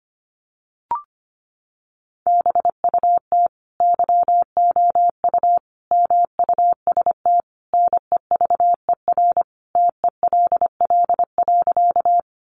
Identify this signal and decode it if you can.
Morse code, heard as a keyed tone, about 25 wpm, 700 Hz. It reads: BUT YOU MUST NEVER TELL.
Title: Sentences from Top 300 Words 25wpm